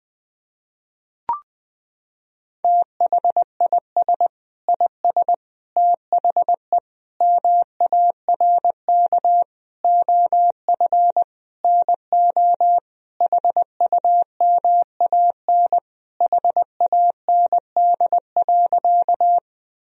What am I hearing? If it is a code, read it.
Morse code, heard as a keyed tone, about 20 wpm, 700 Hz. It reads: THIS IS THE MARK OF NO HUMAN HAND.
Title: Sentences from Top 1000 Words - Review Part 3 - 20wpm